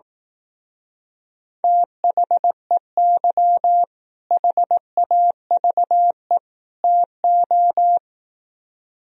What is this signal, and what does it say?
Morse code, heard as a keyed tone, about 18 wpm, 700 Hz. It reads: THEY HAVE TO
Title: Speed-Racing - Sets of 3 Words 15wpm